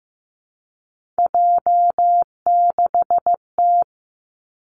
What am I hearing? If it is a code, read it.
Morse code, heard as a keyed tone, about 15 wpm, 700 Hz. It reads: J6T
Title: Three Challenging Letter-Number Combinations 15wpm